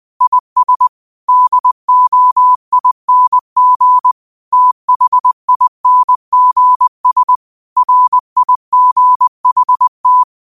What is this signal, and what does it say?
Morse code, heard as a keyed tone, about 20 wpm, 1000 Hz. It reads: IS DOING THINGS RIGHT